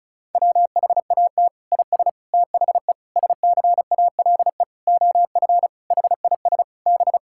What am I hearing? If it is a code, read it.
Morse code, heard as a keyed tone, about 34 wpm, 700 Hz. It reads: WHAT IS THE SCALE OF HIS B
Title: Sentences from Top 1000 Words - Review Part 5 - 35wpm